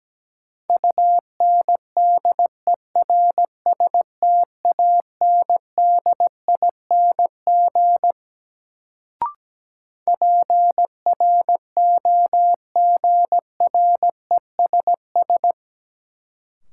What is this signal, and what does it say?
Morse code, heard as a keyed tone, about 17 wpm, 700 Hz. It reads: UNDERSTANDING PROGRESS
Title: Sentences from Top 1400 Words - New Word Drill Alt - 17wpm